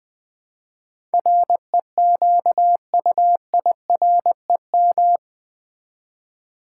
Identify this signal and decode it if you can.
Morse code, heard as a keyed tone, about 21 wpm, 700 Hz. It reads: REQUIREM
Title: Build A Word - 4 Letters At A Time - 11-Letter Words - 20wpm